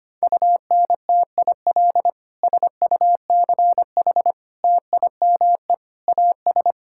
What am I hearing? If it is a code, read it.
Morse code, heard as a keyed tone, about 25 wpm, 700 Hz. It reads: UNTIL SUCH TIME AS
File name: practice / 25wpm / Sentences with 9-Word Phrases - 25wpm